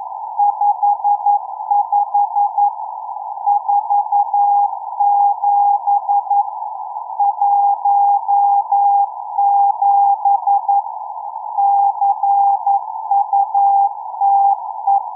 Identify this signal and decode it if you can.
Morse code, heard as a keyed tone, about 11 wpm, 780 Hz. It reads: 55 47 17 CUTE